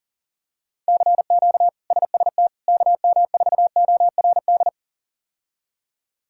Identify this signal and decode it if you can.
Morse code, heard as a keyed tone, about 40 wpm, 700 Hz. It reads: CQ SST KM4ORD
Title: K1USN SST QSO Practice - 40wpm